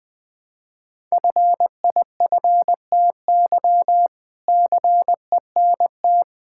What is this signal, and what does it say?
Morse code, heard as a keyed tone, about 21 wpm, 700 Hz. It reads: FIFTY CENT
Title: All-Age Appropriate Jokes - 20wpm